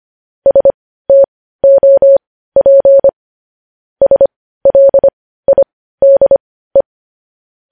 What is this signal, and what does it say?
Morse code, heard as a keyed tone, about 25 wpm, spaced as for 15 wpm, 550 Hz. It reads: STOP SLIDE
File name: podcast / News headlines-1x-25x15